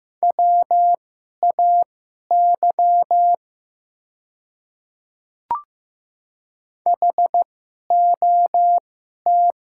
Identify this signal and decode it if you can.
Morse code, heard as a keyed tone, about 15 wpm, spaced as for 11 wpm, 700 Hz. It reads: WAY HOT
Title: 2x Character Spacing - Top 100 Words - 15wpm